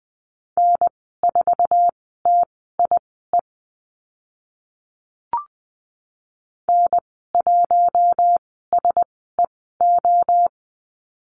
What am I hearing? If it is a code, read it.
Morse code, heard as a keyed tone, about 20 wpm, spaced as for 14 wpm, 700 Hz. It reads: N4TIE N1SEO
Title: 076 Lesson 13g - Letter L - Call Signs 20wpm